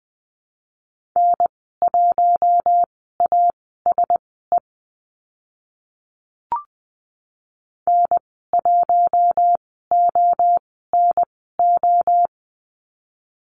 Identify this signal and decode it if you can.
Morse code, heard as a keyed tone, about 20 wpm, spaced as for 14 wpm, 700 Hz. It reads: N1ASE N1ONO
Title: 048 Lesson 9g - Number 4 - Call Signs 20wpm